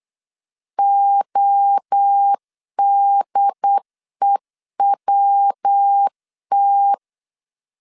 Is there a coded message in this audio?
Yes: Morse code from 0.8 to 6.9 s, heard as a keyed tone, about 8 wpm, 800 Hz. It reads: ODEWT